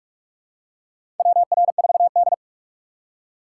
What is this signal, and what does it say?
Morse code, heard as a keyed tone, about 45 wpm, 700 Hz. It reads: WR4D
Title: USA Call Signs - 4 Characters - 4x Character Spacing - 45wpm